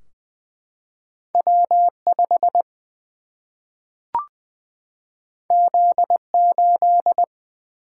Morse code, heard as a keyed tone, about 20 wpm, 700 Hz. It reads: W5 Z8
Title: Single-Letter Single-Number 20wpm